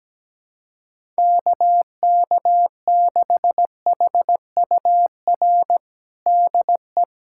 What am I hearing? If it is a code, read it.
Morse code, heard as a keyed tone, about 17 wpm, 700 Hz. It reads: KK6HUR DE
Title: USA Callsign DE Callsign - 6 Characters - 17wpm